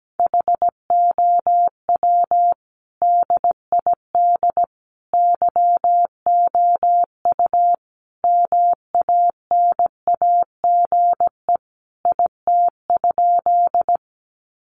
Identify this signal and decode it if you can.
Morse code, heard as a keyed tone, about 17 wpm, 700 Hz. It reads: HOW DID YOU MANAGE IT?
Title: Sentences from Top 1400 Words 17wpm